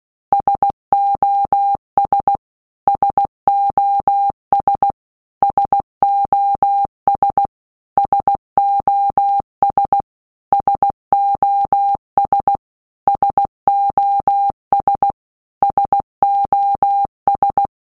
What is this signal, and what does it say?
Morse code, heard as a keyed tone, about 16 wpm, 800 Hz. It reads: SOS SOS SOS SOS SOS SOS SOS
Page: SOS sound in Morse Code (Slow)